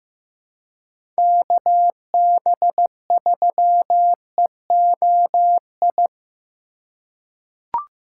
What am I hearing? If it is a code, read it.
Morse code, heard as a keyed tone, about 16 wpm, 700 Hz. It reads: KB3EOI
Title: USA Call Signs - 6 Characters - 4x Character Spacing - 15wpm